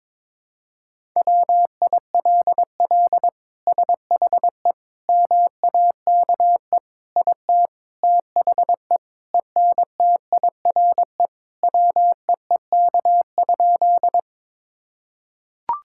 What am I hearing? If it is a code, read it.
Morse code, heard as a keyed tone, about 22 wpm, 700 Hz. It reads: WILL SHE MAKE IT THE ENTIRE WEEK?
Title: Sentences from Top 1000 Words - Review Part 5 - 22wpm